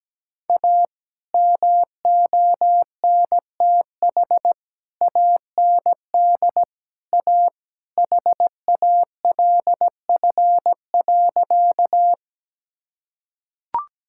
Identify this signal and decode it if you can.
Morse code, heard as a keyed tone, about 17 wpm, 700 Hz. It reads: A MONTH AND A HALF.
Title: Sentences from Top 1000 Words - Review Part 5 - 17wpm